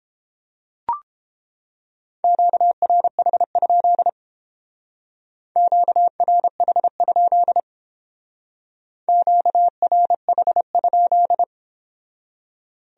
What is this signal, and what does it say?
Morse code, heard as a keyed tone, about 28 wpm, 700 Hz. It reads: QRH? QRH? QRH?